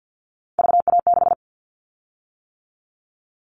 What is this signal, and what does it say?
Morse code, heard as a keyed tone, about 50 wpm, 700 Hz. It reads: VA6
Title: Callsign Prefixes for Top 50 DXCC - 50wpm